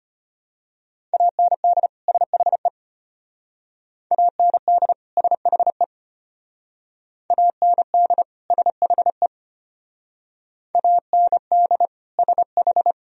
Morse code, heard as a keyed tone, about 31 wpm, 700 Hz. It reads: AND SHE AND SHE AND SHE AND SH
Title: Speed-Racing - Sets of 2 Words 25wpm